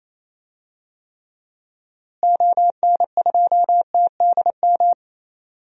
Morse code, heard as a keyed tone, about 28 wpm, 700 Hz. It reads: ON2TDM
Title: DXCC-100 Call Signs - 6 Characters - Repeated 2x - 28wpm